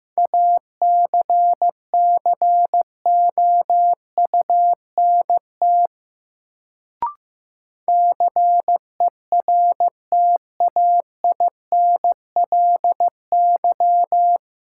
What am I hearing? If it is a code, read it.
Morse code, heard as a keyed tone, about 15 wpm, 700 Hz. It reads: ACCOUNT CERTAINLY
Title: Sentences from Top 900 Words - New Word Drill - 15wpm